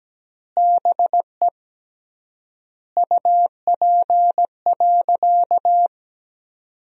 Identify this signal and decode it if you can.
Morse code, heard as a keyed tone, about 17 wpm, 700 Hz. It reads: BE UP.
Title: 3x Word-Spacing - Sentences from Top 300 Words - 17wpm